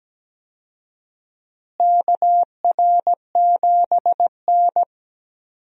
Morse code, heard as a keyed tone, about 17 wpm, 700 Hz. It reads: KR7N
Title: USA Call Signs - 4 Characters - Repeated 2x - 17wpm